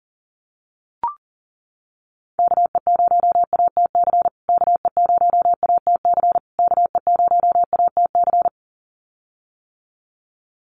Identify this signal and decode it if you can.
Morse code, heard as a keyed tone, about 40 wpm, 700 Hz. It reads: KE0ATC KE0ATC KE0ATC
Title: USA Call Signs Repeated 3x 40wpm